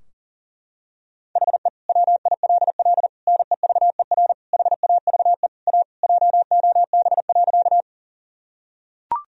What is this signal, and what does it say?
Morse code, heard as a keyed tone, about 40 wpm, 700 Hz. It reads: HE WILL NEVER HAVE A JOB.
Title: Sentences from Top 1000 Words - Review Part 3 - 40wpm